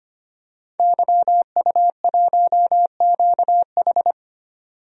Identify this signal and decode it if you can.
Morse code, heard as a keyed tone, about 25 wpm, 700 Hz. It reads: YU1QH
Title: DXCC-100 Call Signs - 5 Characters - 25wpm